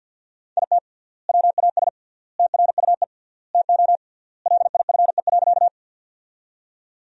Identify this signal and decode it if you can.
Morse code, heard as a keyed tone, about 50 wpm, 700 Hz. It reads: IT WAS TRUE TO LIFE.
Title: 3x Word-Spacing - Sentences from Top 400 Words - 50wpm